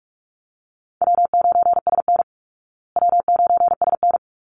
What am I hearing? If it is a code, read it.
Morse code, heard as a keyed tone, about 44 wpm, 700 Hz. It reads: W9SN W9SN
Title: Call Signs Repeated 3x - 45wpm